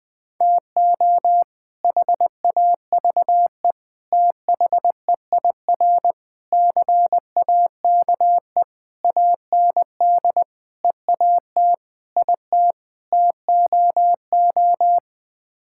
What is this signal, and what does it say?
Morse code, heard as a keyed tone, about 19 wpm, 700 Hz. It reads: TO HAVE THEIR CAKE AND EAT IT TOO
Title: Sets of 8 Words - 20wpm